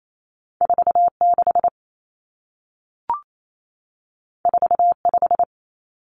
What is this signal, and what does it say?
Morse code, heard as a keyed tone, about 28 wpm, 700 Hz. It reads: 46 45
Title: Common Cut Numbers - 2-Digit 28wpm